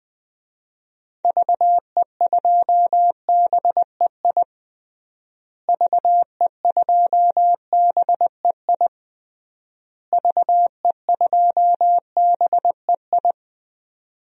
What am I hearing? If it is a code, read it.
Morse code, heard as a keyed tone, about 20 wpm, 700 Hz. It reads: VE2BEI VE2BEI VE2BEI
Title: DXCC-100 Call Signs - 6 Characters - Repeated 3x - 20wpm